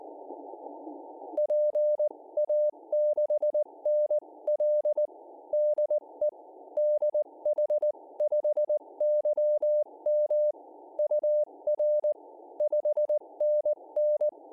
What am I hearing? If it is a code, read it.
Morse code, heard as a keyed tone, about 19 wpm, 600 Hz. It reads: PA6NL DE DH5YM UR 5NN